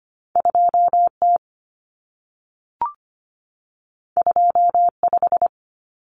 Morse code, heard as a keyed tone, about 25 wpm, 700 Hz. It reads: 2T 25
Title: Common Cut Numbers - 2-Digit 25wpm